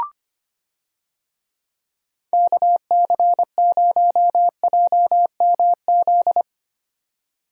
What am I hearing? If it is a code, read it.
Morse code, heard as a keyed tone, about 25 wpm, 700 Hz. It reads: KC0JMZ
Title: USA Call Signs - 6 Characters - 25wpm